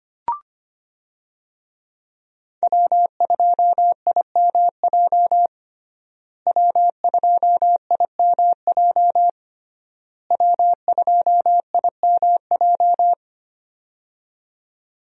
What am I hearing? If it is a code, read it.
Morse code, heard as a keyed tone, about 25 wpm, 700 Hz. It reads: W2IMJ W2IMJ W2IMJ
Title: USA Call Signs - 5 Characters - Repeated 3x - 25wpm